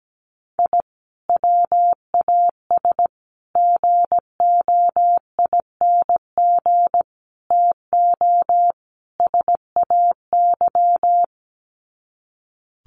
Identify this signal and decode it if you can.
Morse code, heard as a keyed tone, about 17 wpm, 700 Hz. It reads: I WAS GOING TO SAY
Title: Sets of 5 Words 17wpm